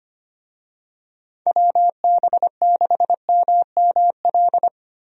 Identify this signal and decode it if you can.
Morse code, heard as a keyed tone, about 25 wpm, 700 Hz. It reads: WB6MML
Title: USA Call Signs - 6 Characters - Repeated 3x - 25wpm